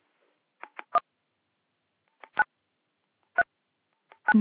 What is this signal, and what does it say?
DTMF, heard as telephone touch tones, tones of about 60 ms. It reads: 9 3 9